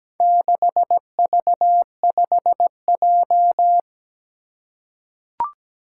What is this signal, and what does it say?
Morse code, heard as a keyed tone, about 17 wpm, 700 Hz. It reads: 6V5J
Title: Four Challenging Letter-Number Combinations 17wpm